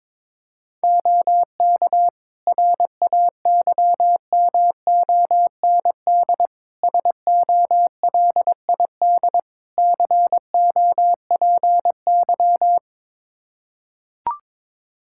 Morse code, heard as a keyed tone, about 22 wpm, 700 Hz. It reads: OK RAYMOND SOLID COPY
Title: Practice QSO - Exchange 3 - More Info - 22wpm